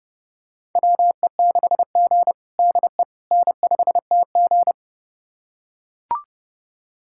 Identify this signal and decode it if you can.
Morse code, heard as a keyed tone, about 30 wpm, 700 Hz. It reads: WE6G DE N5TG
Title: USA Callsign DE Callsign - 4 Characters - 30wpm